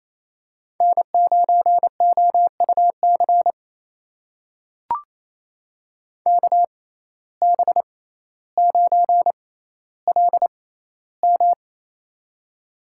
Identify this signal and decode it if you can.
Morse code, heard as a keyed tone, about 28 wpm, spaced as for 14 wpm, 700 Hz. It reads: N9OUC KB9LM
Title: USA Call Signs - 5 Characters - 6x Character Spacing - 28wpm